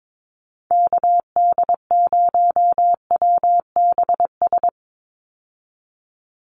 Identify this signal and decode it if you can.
Morse code, heard as a keyed tone, about 22 wpm, 700 Hz. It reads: KD0WBS
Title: USA Call Signs 22wpm